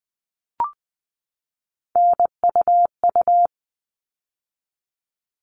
Morse code, heard as a keyed tone, about 20 wpm, 700 Hz. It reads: NUU
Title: All Cut Numbers - 3-Digit 20wpm